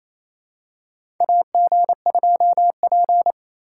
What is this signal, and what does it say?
Morse code, heard as a keyed tone, about 28 wpm, 700 Hz. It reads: AG2P
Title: USA Call Signs - 4 Characters - 4x Character Spacing - 28wpm